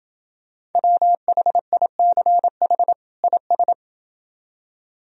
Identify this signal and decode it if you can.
Morse code, heard as a keyed tone, about 29 wpm, 700 Hz. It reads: WHICH IS